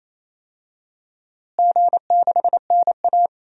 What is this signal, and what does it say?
Morse code, heard as a keyed tone, about 28 wpm, 700 Hz. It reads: G6NA
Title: DXCC-100 Call Signs - 4 Characters - Repeated 2x - 28wpm